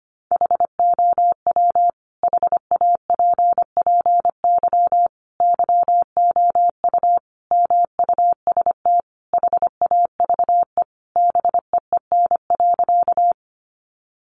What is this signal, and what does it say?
Morse code, heard as a keyed tone, about 25 wpm, 700 Hz. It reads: HOW HAPPY YOU MUST HAVE BEEN.
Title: Sentences from Top 1000 Words 25wpm